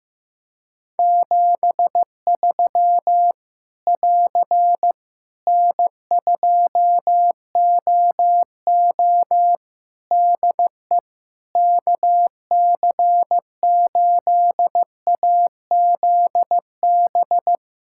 Morse code, heard as a keyed tone, about 15 wpm, 700 Hz. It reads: 73 <AR> N2OO DE KC8AZB
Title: Practice QSO - Exchange 4 - Ending - 15wpm